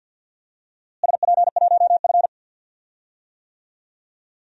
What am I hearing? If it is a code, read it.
Morse code, heard as a keyed tone, about 47 wpm, 700 Hz. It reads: SP1F
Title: DXCC-100 Call Signs - 4 Characters - Repeated 2x - 50wpm